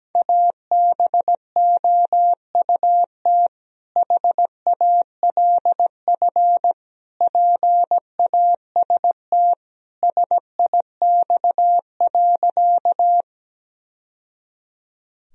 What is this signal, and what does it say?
Morse code, heard as a keyed tone, about 17 wpm, 700 Hz. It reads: ABOUT HALF PAST SIX.
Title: Sentences from Top 1000 Words - Review Part 3 - 17wpm